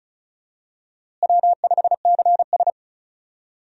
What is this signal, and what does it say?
Morse code, heard as a keyed tone, about 36 wpm, 700 Hz. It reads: W5CS
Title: USA Call Signs - 4 Characters - Repeated 2x - 35wpm